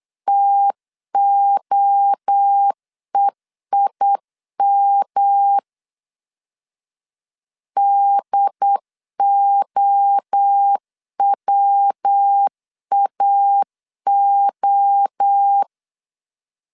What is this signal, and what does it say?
Morse code, heard as a keyed tone, about 8 wpm, 800 Hz. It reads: TOEIM DOWAO
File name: CW / DET / 1 3 1500C S B 8wpm 4141